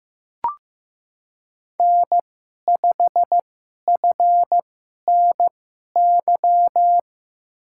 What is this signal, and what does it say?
Morse code, heard as a keyed tone, about 15 wpm, spaced as for 11 wpm, 700 Hz. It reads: N5FNY
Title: USA Call Signs - 5 Characters - 2x Character Spacing - 15wpm